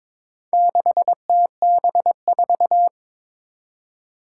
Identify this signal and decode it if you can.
Morse code, heard as a keyed tone, about 22 wpm, 700 Hz. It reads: 6TB4
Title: Four Challenging Letter-Number Combinations 22wpm